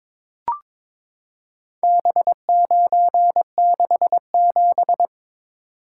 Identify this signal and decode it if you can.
Morse code, heard as a keyed tone, about 22 wpm, 700 Hz. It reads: B967
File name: practice / 22wpm / Four Letter-Number Combinations - 22wpm